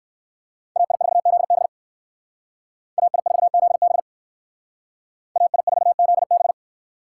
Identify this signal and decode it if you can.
Morse code, heard as a keyed tone, about 59 wpm, 700 Hz. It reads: AI4ZB AI4ZB AI4ZB